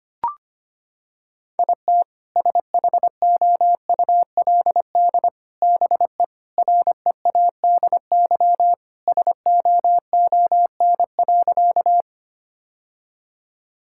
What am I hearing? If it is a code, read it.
Morse code, heard as a keyed tone, about 25 wpm, 700 Hz. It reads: IT SHOULD BE READY SOON.
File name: practice / 25wpm / Sentences with 2-Word Phrases - 25wpm